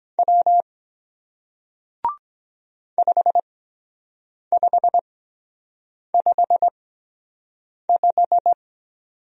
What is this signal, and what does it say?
Morse code, heard as a keyed tone, about 22 wpm, spaced as for 7 wpm, 700 Hz. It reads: W 5555